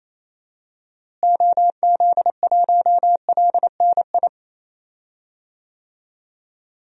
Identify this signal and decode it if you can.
Morse code, heard as a keyed tone, about 27 wpm, 700 Hz. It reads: OZ1LNI
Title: DXCC-100 Call Signs - 6 Characters - Repeated 3x - 28wpm